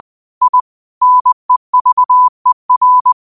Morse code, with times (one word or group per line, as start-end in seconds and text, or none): 0.4-0.6 I
1.0-3.1 NEVER